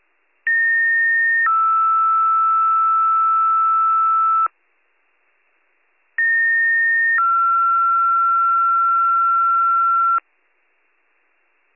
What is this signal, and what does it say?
Two-tone page, heard as a one-second tone, then a three-second tone, about 1850 then 1320 Hz, about 1850 then 1360 Hz. It is heard from 0.5 to 10.2 s.